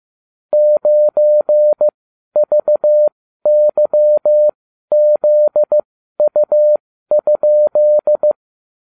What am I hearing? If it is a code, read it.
Morse code, heard as a keyed tone, about 15 wpm, spaced as for 12 wpm, 600 Hz. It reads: 9VYZU?